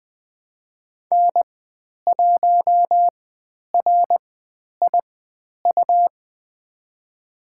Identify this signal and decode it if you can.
Morse code, heard as a keyed tone, about 20 wpm, spaced as for 10 wpm, 700 Hz. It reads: N1RIU
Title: 109 Lesson 18g - Letter m - Call Signs 20wpm